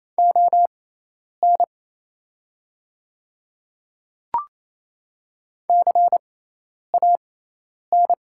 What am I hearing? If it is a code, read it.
Morse code, heard as a keyed tone, about 27 wpm, spaced as for 10 wpm, 690 Hz. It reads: ON CAN
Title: 6x Character Spacing - Top 100 Words - 28wpm